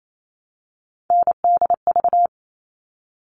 Morse code, heard as a keyed tone, about 28 wpm, 700 Hz. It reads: NDV
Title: All Cut Numbers - 3-Digit 28wpm